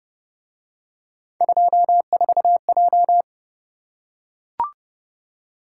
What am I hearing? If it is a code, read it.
Morse code, heard as a keyed tone, about 26 wpm, 700 Hz. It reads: 24J